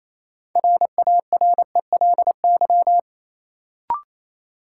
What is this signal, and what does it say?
Morse code, heard as a keyed tone, about 28 wpm, 700 Hz. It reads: RARELY
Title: Sentences from Top 1400 Words - New Word Drill Alt - 28wpm